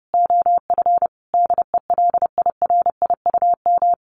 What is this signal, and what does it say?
Morse code, heard as a keyed tone, about 30 wpm, 700 Hz. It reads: OF DELIRIUM
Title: A Sound Of Thunder 30wpm